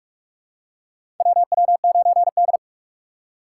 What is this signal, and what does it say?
Morse code, heard as a keyed tone, about 43 wpm, 700 Hz. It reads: WW9D
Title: USA Call Signs - 4 Characters - 3x Character Spacing - 45wpm